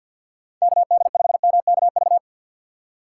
Morse code, heard as a keyed tone, about 52 wpm, 690 Hz. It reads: KD5MKV
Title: USA Call Signs - 6 Characters - 3x Character Spacing - 50wpm